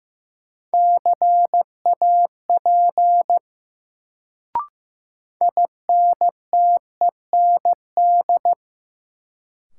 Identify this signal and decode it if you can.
Morse code, heard as a keyed tone, about 15 wpm, 700 Hz. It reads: CAP INTEND
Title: Sentences from Top 1600 Words - New Word Drill Alt - 15wpm